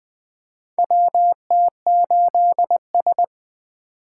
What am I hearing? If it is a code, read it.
Morse code, heard as a keyed tone, about 20 wpm, 700 Hz. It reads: WT8S
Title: USA Call Signs - 4 Characters - 8x Character Spacing - 20wpm